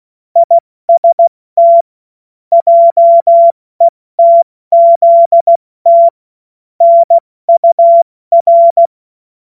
Morse code, heard as a keyed tone, about 16 wpm, 680 Hz. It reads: IST JETZT NUR
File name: Morsekurs / Witze / Witz 17.Oktober 2023 70 F80